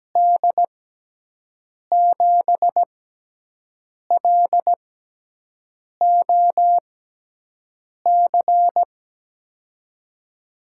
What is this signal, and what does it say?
Morse code, heard as a keyed tone, about 17 wpm, spaced as for 6 wpm, 700 Hz. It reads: D7LOC
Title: USA Call Signs - 6 Characters - 6x Character Spacing - 17wpm